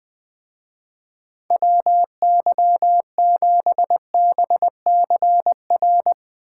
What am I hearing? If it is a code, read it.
Morse code, heard as a keyed tone, about 20 wpm, 700 Hz. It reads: WY7BCR